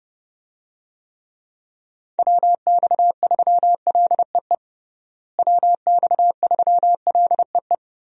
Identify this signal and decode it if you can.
Morse code, heard as a keyed tone, about 30 wpm, 700 Hz. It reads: WX3LEE WX3LEE
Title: USA Call Signs - 6 Characters - Repeated 2x - 30wpm